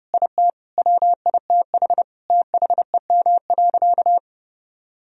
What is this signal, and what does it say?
Morse code, heard as a keyed tone, about 31 wpm, 700 Hz. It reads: IT WITH THEM.
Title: Sentences from Top 1000 Words - Review Part 1 - 30wpm